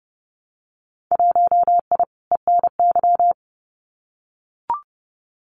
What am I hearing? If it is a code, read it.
Morse code, heard as a keyed tone, about 30 wpm, 700 Hz. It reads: 1I ENY
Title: Winter Field Day Exchanges Encore 30wpm